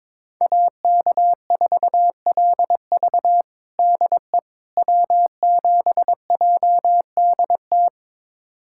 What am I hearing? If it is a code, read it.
Morse code, heard as a keyed tone, about 22 wpm, 700 Hz. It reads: AK4LV DE W7JDT
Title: USA Callsign DE Callsign - 5 Characters - 22wpm